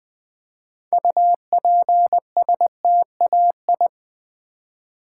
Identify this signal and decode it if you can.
Morse code, heard as a keyed tone, about 20 wpm, 700 Hz. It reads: UPSTAI
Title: Build A Word - 3 Letters At A Time - 8-Letter Words - 20wpm